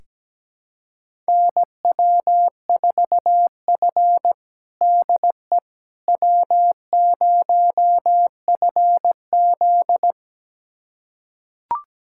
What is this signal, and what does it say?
Morse code, heard as a keyed tone, about 17 wpm, 700 Hz. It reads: NW4F DE W0FZ
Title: USA Callsign DE Callsign - 4 Characters - 17wpm